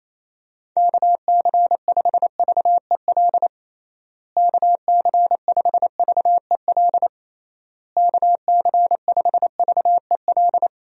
Morse code, heard as a keyed tone, about 27 wpm, 700 Hz. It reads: KC5VEL KC5VEL KC5VEL
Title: USA Call Signs - 6 Characters - Repeated 3x - 28wpm